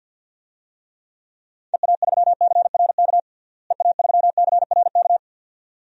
Morse code, heard as a keyed tone, about 49 wpm, 700 Hz. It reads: EA3CRK EA3CRK
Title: DXCC-100 Call Signs - 6 Characters - Repeated 2x - 50wpm